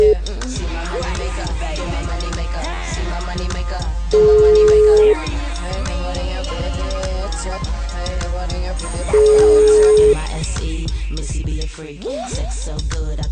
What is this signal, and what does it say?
Ringback tone (Europe, 425 Hz), with speech.